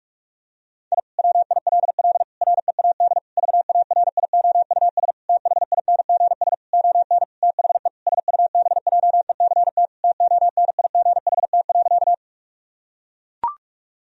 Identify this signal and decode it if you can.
Morse code, heard as a keyed tone, about 41 wpm, 700 Hz. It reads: I WILL READ VARIOUS THINGS ON THE SUBJECT TONIGHT.
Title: Sentences from Top 1000 Words - Review Part 4 - 45wpm